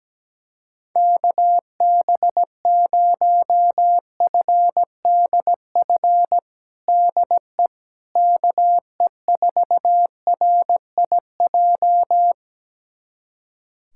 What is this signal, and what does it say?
Morse code, heard as a keyed tone, about 17 wpm, 700 Hz. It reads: KB0FDF DE KE4RIJ
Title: USA Callsign DE Callsign 17wpm